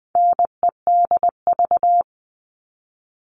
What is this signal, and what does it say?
Morse code, heard as a keyed tone, about 20 wpm, 700 Hz. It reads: NEDV
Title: All Cut Numbers - 4-Digit 20wpm